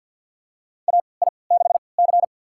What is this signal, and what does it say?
Morse code, heard as a keyed tone, about 48 wpm, spaced as for 28 wpm, 700 Hz. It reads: AI6C
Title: USA Call Signs - 4 Characters - 3x Character Spacing - 50wpm